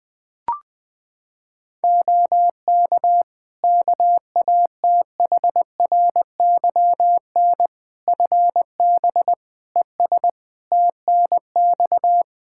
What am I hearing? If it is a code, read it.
Morse code, heard as a keyed tone, about 20 wpm, 700 Hz. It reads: OK KATHRYN FB ES TNX
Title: Practice QSO - Exchange 2 - Info - 20wpm